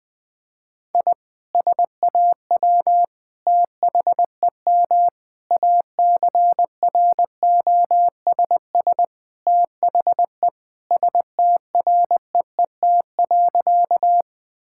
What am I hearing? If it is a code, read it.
Morse code, heard as a keyed tone, about 20 wpm, 700 Hz. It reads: I SAW THEM ACROSS THE STREET.
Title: Sentences from Top 1000 Words - Review Part 5 - 20wpm